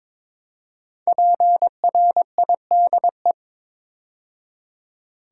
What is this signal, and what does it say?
Morse code, heard as a keyed tone, about 22 wpm, 700 Hz. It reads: PRIDE
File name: 5 Letter Words - First 3 Letters Given - 22wpm